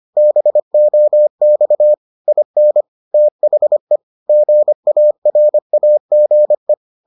Morse code, heard as a keyed tone, about 25 wpm, 600 Hz. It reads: BOX IN THE GARAGE